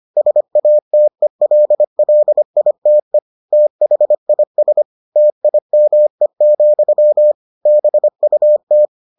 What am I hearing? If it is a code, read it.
Morse code, heard as a keyed tone, about 25 wpm, 600 Hz. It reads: SATELLITE THIS TIME, BUT